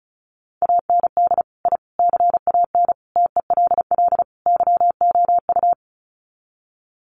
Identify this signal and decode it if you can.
Morse code, heard as a keyed tone, about 34 wpm, 700 Hz. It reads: AND I CAN TELL YOU
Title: Sets of 5 Words Encore2 35wpm